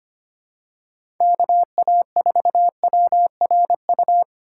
Morse code, heard as a keyed tone, about 25 wpm, 700 Hz. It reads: KA4WRU
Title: USA Call Signs - 6 Characters - 4x Character Spacing - 25wpm